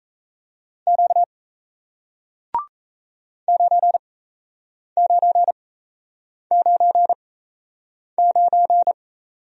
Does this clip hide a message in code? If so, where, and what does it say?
Yes: Morse code from 0.9 to 8.9 s, heard as a keyed tone, about 37 wpm, spaced as for 8 wpm, 700 Hz. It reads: Q 9999